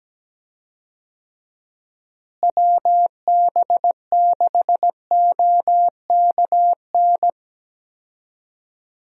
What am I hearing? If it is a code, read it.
Morse code, heard as a keyed tone, about 17 wpm, 700 Hz. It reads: WB6OKN
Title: USA Call Signs - 6 Characters - Repeated 3x - 17wpm